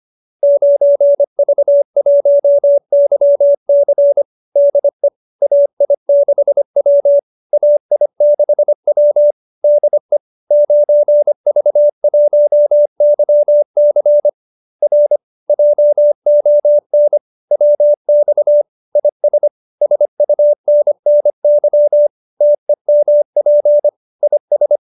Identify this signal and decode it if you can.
Morse code, heard as a keyed tone, about 25 wpm, 590 Hz. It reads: 9V1YC DE AI6W AI6W DE 9V1YC R JON WX IS SUNNY TEMP IS